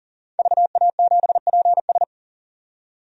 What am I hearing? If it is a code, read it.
Morse code, heard as a keyed tone, about 41 wpm, 700 Hz. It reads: VA7PS
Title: DXCC-100 Call Signs - 5 Characters - 40wpm